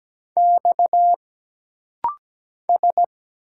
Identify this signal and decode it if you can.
Morse code, heard as a keyed tone, about 17 wpm, 700 Hz. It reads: X S
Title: Single Letters - Rapid-Fire 17wpm